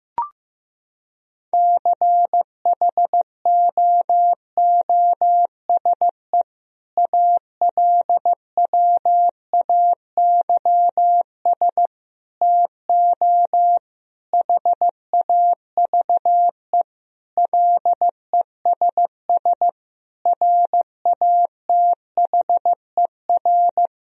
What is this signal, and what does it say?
Morse code, heard as a keyed tone, about 15 wpm, 700 Hz. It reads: CHOOSE ALWAYS TO HAVE LESS RATHER
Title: Sentences from Top 1000 Words - Review Part 4 - 15wpm